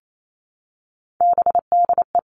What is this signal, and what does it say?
Morse code, heard as a keyed tone, about 28 wpm, 700 Hz. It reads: BDE
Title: Three Challenging Letter-Number Combinations 28wpm